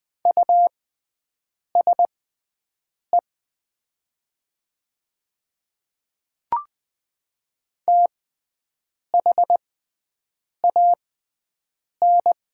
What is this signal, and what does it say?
Morse code, heard as a keyed tone, about 20 wpm, spaced as for 7 wpm, 700 Hz. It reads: USE THAN
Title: 6x Character Spacing - Top 100 Words - 20wpm